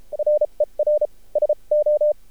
Morse code, heard as a keyed tone, about 30 wpm, 600 Hz. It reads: RER SO
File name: archived / CRSnet